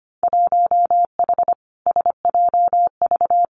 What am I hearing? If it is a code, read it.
Morse code, heard as a keyed tone, about 25 wpm, 700 Hz. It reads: 1H SJV
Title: Winter Field Day Exchanges 25wpm